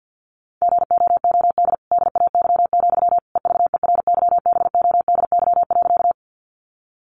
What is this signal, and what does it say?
Morse code, heard as a keyed tone, about 50 wpm, 700 Hz. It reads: GOOD DAY, EVERYBODY.
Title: Sentences from Top 1400 Words 50wpm